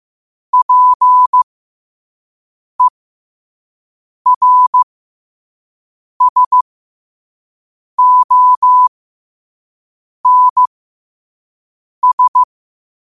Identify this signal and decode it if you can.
Morse code, heard as a keyed tone, about 15 wpm, spaced as for 5 wpm, 1000 Hz. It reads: PERSONS
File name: qotdfiles / 5wpm